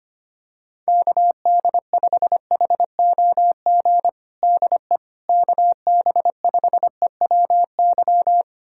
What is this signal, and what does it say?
Morse code, heard as a keyed tone, about 25 wpm, 700 Hz. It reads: KD5HOG DE KB5EWY
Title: USA Callsign DE Callsign - 6 Characters - 25wpm